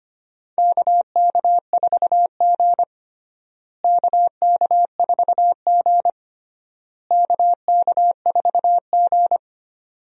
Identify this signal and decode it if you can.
Morse code, heard as a keyed tone, about 25 wpm, 700 Hz. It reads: KK4G KK4G KK4G
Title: USA Call Signs - 4 Characters - Repeated 3x - 25wpm